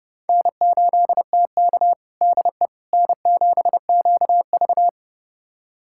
Morse code, heard as a keyed tone, about 31 wpm, 700 Hz. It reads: N8TK DE N7QV
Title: USA Callsign DE Callsign - 4 Characters - 30wpm